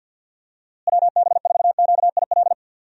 Morse code, heard as a keyed tone, about 52 wpm, 700 Hz. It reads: WB4QIL
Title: USA Call Signs - 6 Characters - Repeated 3x - 50wpm